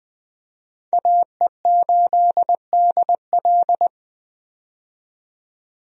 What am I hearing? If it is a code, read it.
Morse code, heard as a keyed tone, about 20 wpm, 700 Hz. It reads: AE8DL
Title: USA Call Signs - 5 Characters - 20wpm